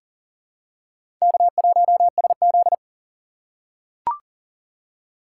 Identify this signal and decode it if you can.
Morse code, heard as a keyed tone, about 40 wpm, 700 Hz. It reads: K1SZ
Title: USA Call Signs - 4 Characters - 6x Character Spacing - 40wpm